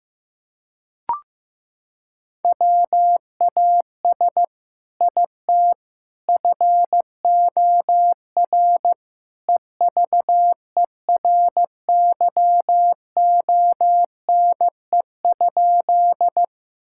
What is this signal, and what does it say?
Morse code, heard as a keyed tone, about 15 wpm, 700 Hz. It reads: WAS IT FOR EVERYONE?
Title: Sentences from Top 1000 Words - Review Part 5 - 15wpm